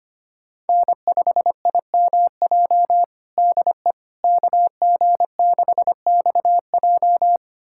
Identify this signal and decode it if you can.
Morse code, heard as a keyed tone, about 25 wpm, 700 Hz. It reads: N5IMJ DE KG6XJ
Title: USA Callsign DE Callsign - 5 Characters - 25wpm